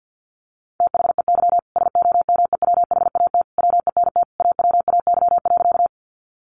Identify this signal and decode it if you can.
Morse code, heard as a keyed tone, about 50 wpm, 700 Hz. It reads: THEY SOMEWHAT WENT AWAY.
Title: Sentences from Top 1300 Words 50wpm